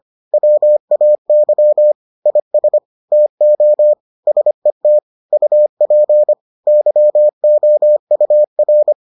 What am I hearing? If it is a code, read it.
Morse code, heard as a keyed tone, about 25 wpm, 600 Hz. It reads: WAY IS TO SET UP YOUR